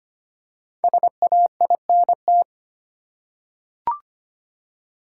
Morse code, heard as a keyed tone, about 25 wpm, 700 Hz. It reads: SAINT
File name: 5 Letter Words - First 3 Letters Given - 25wpm